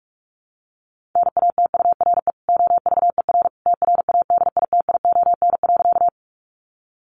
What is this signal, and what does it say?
Morse code, heard as a keyed tone, about 44 wpm, 700 Hz. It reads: NATURE OVER TRADITION.
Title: Sentences from Top 1500 Words 45wpm